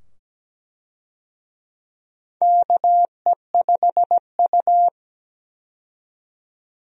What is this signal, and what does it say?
Morse code, heard as a keyed tone, about 17 wpm, 700 Hz. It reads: KE5U
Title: USA Call Signs - 4 Characters - 17wpm